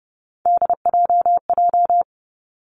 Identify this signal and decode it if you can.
Morse code, heard as a keyed tone, about 30 wpm, 700 Hz. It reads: DJJ
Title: Three Challenging Letter-Number Combinations 30wpm